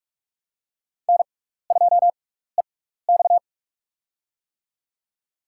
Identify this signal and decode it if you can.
Morse code, heard as a keyed tone, about 41 wpm, spaced as for 15 wpm, 700 Hz. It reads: N2EX
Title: USA Call Signs - 4 Characters - 6x Character Spacing - 45wpm